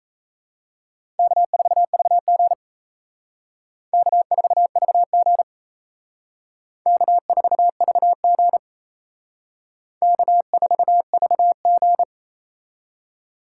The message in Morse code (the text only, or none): K4VG K4VG K4VG K4VG